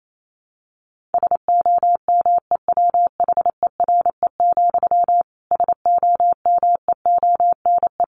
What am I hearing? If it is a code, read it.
Morse code, heard as a keyed tone, about 28 wpm, 700 Hz. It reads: SOMEWHERE, SOMEONE
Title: A Sound Of Thunder 28wpm